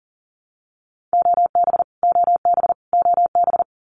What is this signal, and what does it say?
Morse code, heard as a keyed tone, about 40 wpm, 700 Hz. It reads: OB OB OB
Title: Abbreviations Repeated 3x - 40wpm